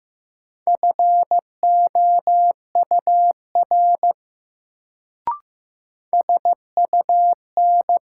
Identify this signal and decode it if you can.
Morse code, heard as a keyed tone, about 15 wpm, 700 Hz. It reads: FOUR SUN
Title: Sentences from Top 300 Words - New Word Drill - 15wpm